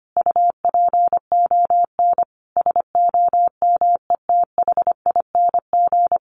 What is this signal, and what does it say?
Morse code, heard as a keyed tone, about 25 wpm, 700 Hz. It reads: UPON SOMETHING
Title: Sentences from Top 1400 Words 25wpm